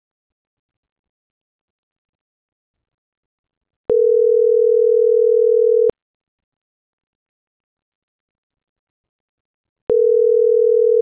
Ringback tone (North America), 440 + 480 Hz.
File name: CENTRAL GENERAL / out deenis.chavez